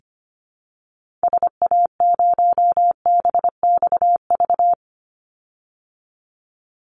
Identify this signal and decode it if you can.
Morse code, heard as a keyed tone, about 25 wpm, 700 Hz. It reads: SA0BXV
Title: Call Signs Repeated 3x - 25wpm